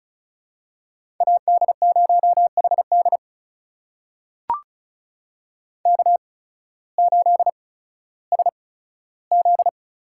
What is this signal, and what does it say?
Morse code, heard as a keyed tone, about 36 wpm, spaced as for 16 wpm, 700 Hz. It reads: AD0HD K8SZ
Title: USA Call Signs - 5 Characters - 8x Character Spacing - 35wpm